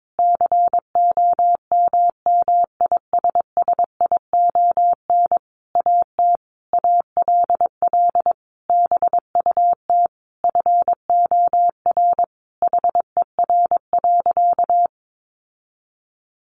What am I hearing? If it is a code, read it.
Morse code, heard as a keyed tone, about 22 wpm, 700 Hz. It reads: COMMISSION AT ALL BUT FOR HER.
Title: Sentences from Top 1400 Words 22wpm